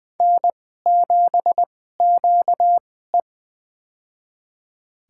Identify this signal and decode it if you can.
Morse code, heard as a keyed tone, about 20 wpm, spaced as for 14 wpm, 700 Hz. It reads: N7QE
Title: USA Call Signs - 4 Characters - 2x Character Spacing - 20wpm